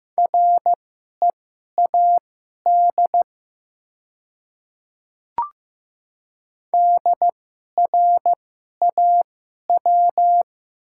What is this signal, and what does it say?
Morse code, heard as a keyed tone, about 15 wpm, spaced as for 11 wpm, 700 Hz. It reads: READ DRAW
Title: 2x Character Spacing - Top 300 Words - 15wpm